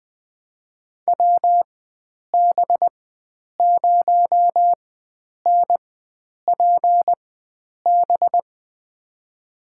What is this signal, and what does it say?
Morse code, heard as a keyed tone, about 20 wpm, spaced as for 9 wpm, 700 Hz. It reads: WB0NPB
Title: USA Call Signs - 4x Character Spacing - 20wpm